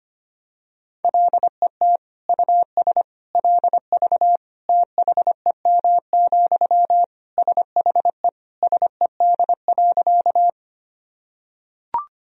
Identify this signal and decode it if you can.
Morse code, heard as a keyed tone, about 25 wpm, 700 Hz. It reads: LET US LV THEM, SHE SED.
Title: Sentences from Top 400 Words and Abbreviations 25wpm